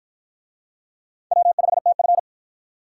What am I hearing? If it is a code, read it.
Morse code, heard as a keyed tone, about 46 wpm, 700 Hz. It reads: W5TF